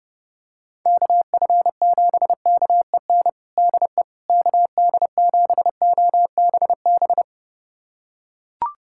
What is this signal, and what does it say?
Morse code, heard as a keyed tone, about 32 wpm, 700 Hz. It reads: KF7KEN DE KD7OBB
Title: USA Callsign DE Callsign - 6 Characters - 30wpm